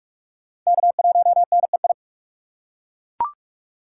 Morse code, heard as a keyed tone, about 40 wpm, 700 Hz. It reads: K1NEI